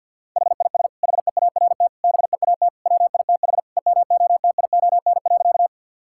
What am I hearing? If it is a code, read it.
Morse code, heard as a keyed tone, about 50 wpm, 700 Hz. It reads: HIS HEART BEAT WITH EMOTION.
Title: Sentences from Top 1600 Words 50wpm